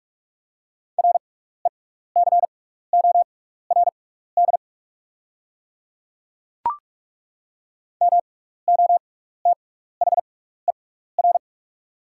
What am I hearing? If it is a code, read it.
Morse code, heard as a keyed tone, about 44 wpm, spaced as for 15 wpm, 690 Hz. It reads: RECORD MOTHER
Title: 6x Character Spacing - Top 500 Words - 45wpm